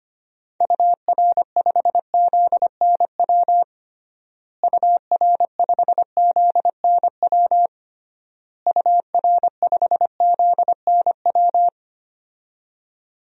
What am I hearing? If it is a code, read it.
Morse code, heard as a keyed tone, about 25 wpm, 700 Hz. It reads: UR5ZNW UR5ZNW UR5ZNW
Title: DXCC-100 Call Signs - 6 Characters - Repeated 3x - 25wpm